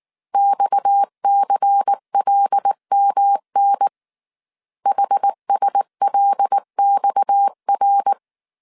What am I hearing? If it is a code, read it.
Morse code, heard as a keyed tone, about 19 wpm, 800 Hz. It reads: XCLMN HSLXR
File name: CW / DET / 1 6 1500C S C 18wpm 2356